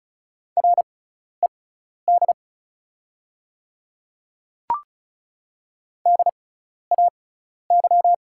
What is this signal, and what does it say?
Morse code, heard as a keyed tone, about 36 wpm, spaced as for 12 wpm, 690 Hz. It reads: RED DAY
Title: 6x Character Spacing - Top 500 Words - 35wpm